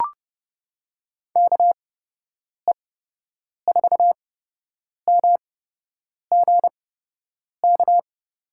Morse code, heard as a keyed tone, about 30 wpm, spaced as for 8 wpm, 700 Hz. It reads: KE4MGK
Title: USA Call Signs - 6 Characters - 8x Character Spacing - 30wpm